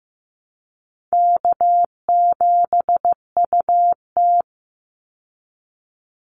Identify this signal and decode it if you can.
Morse code, heard as a keyed tone, about 15 wpm, 700 Hz. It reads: K7UT
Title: Easy Call Signs Repeated 3x - 15wpm